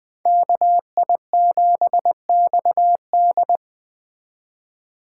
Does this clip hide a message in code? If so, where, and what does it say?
Yes: Morse code from 0.3 to 3.6 s, heard as a keyed tone, about 21 wpm, 700 Hz. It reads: KI7XD